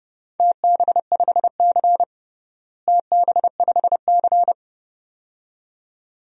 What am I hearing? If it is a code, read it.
Morse code, heard as a keyed tone, about 30 wpm, 700 Hz. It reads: TB5C TB5C
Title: DXCC-100 Call Signs - 4 Characters - Repeated 2x - 30wpm